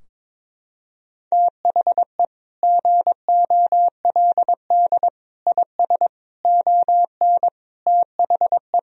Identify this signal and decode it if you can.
Morse code, heard as a keyed tone, about 21 wpm, 700 Hz. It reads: THE GOLD IS ON THE
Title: Sentences from Top 1000 Words - Review Part 3 - 22wpm